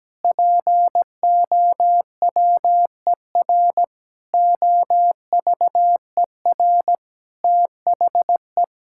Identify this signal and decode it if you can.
Morse code, heard as a keyed tone, about 17 wpm, 700 Hz. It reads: POWER OVER THE
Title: Sentences from Top 1000 Words - Review Part 5 - 17wpm